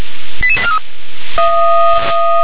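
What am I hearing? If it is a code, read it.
Five-tone selective call (ZVEI), heard as quick runs of five tones, tones of about 70 ms. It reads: 88843
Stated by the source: ZVEI five-tone call 88865 received by SDR.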